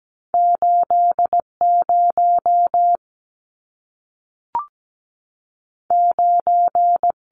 Morse code, heard as a keyed tone, about 17 wpm, 700 Hz. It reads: 80 9
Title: Numbers - 2-Digit 17wpm